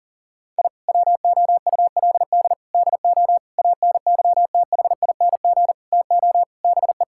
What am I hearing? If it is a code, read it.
Morse code, heard as a keyed tone, about 40 wpm, 700 Hz. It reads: I WOULD DO ANYTHING TO BE
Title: Sentences from Top 1000 Words - Review Part 6 - 40wpm